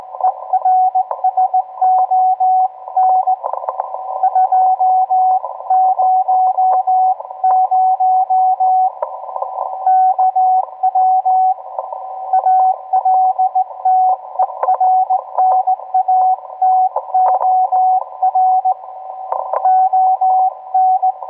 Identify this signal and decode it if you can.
Morse code, heard as a keyed tone, about 17 wpm, 750 Hz, with radio noise. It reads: ERSON 200 KW ALTERNATOR ON